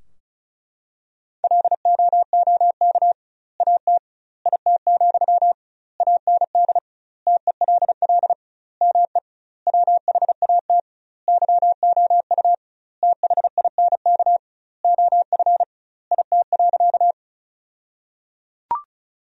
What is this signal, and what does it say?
Morse code, heard as a keyed tone, about 36 wpm, 700 Hz. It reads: LOOK AT IT, AND TELL ME WHAT YOU THINK OF IT.
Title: 2x Word-Spacing - Sentences from Top 200 Words - 35wpm